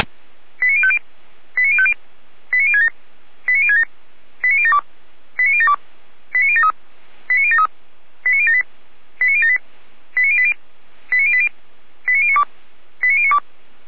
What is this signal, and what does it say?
Five-tone selective call (ZVEI), heard as quick runs of five tones, tones of about 70 ms. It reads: 89050 89050 89067 89067 89072 89072 89073 89073 89078 89078 89080 89080 89092 89092